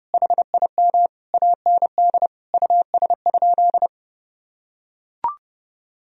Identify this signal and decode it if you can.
Morse code, heard as a keyed tone, about 30 wpm, 700 Hz. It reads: HIM AND US?
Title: Sentences from Top 1000 Words - Review Part 3 - 30wpm